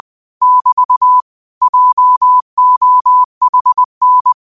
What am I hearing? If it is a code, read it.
Morse code, heard as a keyed tone, about 20 wpm, 1000 Hz. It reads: <BT> JOHN
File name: qotdfiles / 20wpm